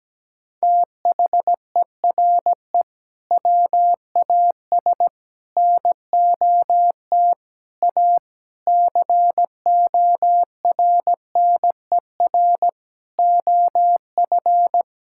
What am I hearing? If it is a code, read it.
Morse code, heard as a keyed tone, about 17 wpm, 700 Hz. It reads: THERE WAS NOT A CORNER OF
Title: Sentences from Top 1600 Words 17wpm